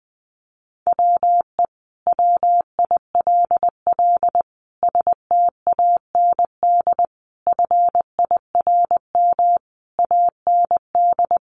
Morse code, heard as a keyed tone, about 20 wpm, 700 Hz. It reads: WE WILL STAND FIRM AND
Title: Sentences from Top 1400 Words 20wpm